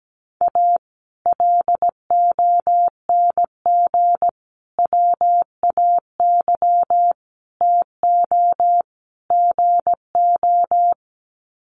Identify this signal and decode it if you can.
Morse code, heard as a keyed tone, about 17 wpm, 700 Hz. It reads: A LONG WAY TO GO
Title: Sets of 5 Words 17wpm